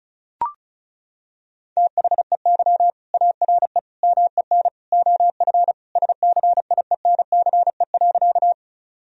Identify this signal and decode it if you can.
Morse code, heard as a keyed tone, about 34 wpm, 700 Hz. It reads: THEY ARE MEN OF SCIENCE.
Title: Sentences from Top 1000 Words - Review Part 2 - 35wpm